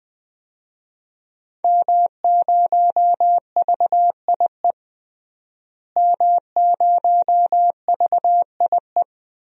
Morse code, heard as a keyed tone, about 21 wpm, 700 Hz. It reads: M0VIE M0VIE
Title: DXCC-100 Call Signs - 5 Characters - Repeated 2x - 20wpm